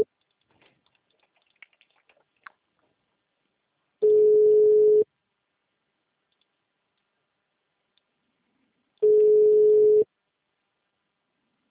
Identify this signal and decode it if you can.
Ringback tone (Europe, 425 Hz).
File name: out- -2002- - - .1542